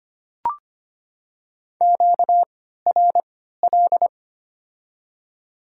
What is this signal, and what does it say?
Morse code, heard as a keyed tone, about 25 wpm, spaced as for 14 wpm, 700 Hz. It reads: QRL
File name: Top 20 QSO Elements - 3x Character Spacing - 25wpm